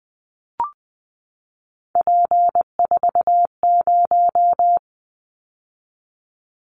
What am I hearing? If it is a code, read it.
Morse code, heard as a keyed tone, about 20 wpm, 700 Hz. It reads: P40
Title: Callsign Prefixes for Top 100 DXCC - 20wpm